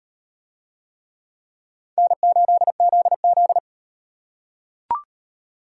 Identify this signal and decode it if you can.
Morse code, heard as a keyed tone, about 38 wpm, 700 Hz. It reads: N8ZZ